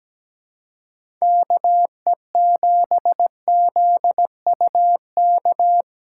Morse code, heard as a keyed tone, about 17 wpm, 700 Hz. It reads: KE7ZUK
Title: USA Call Signs - 6x Character Spacing - 17wpm